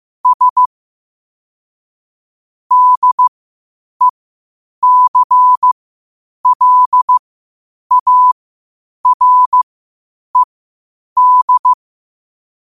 Morse code, heard as a keyed tone, about 15 wpm, spaced as for 8 wpm, 1000 Hz. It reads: S DECLARED
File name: qotdfiles / 8wpm